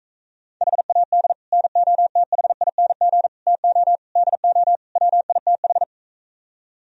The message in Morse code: HAD NOTHING TO DO WITH

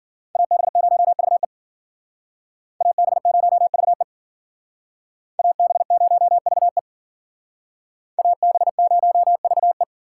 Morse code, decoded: AB0VE AB0VE AB0VE AB0VE

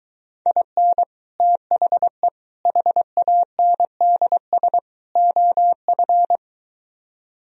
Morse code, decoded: IN THE HANDS OF

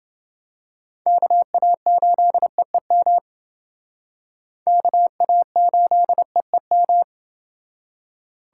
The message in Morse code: KA8EEM KA8EEM

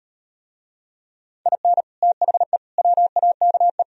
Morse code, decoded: IN THE WAKE

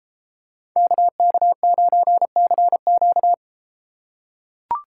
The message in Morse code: KK9CQ

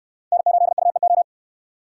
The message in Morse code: N8UEO

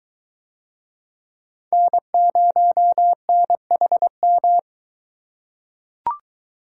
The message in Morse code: N0NHM

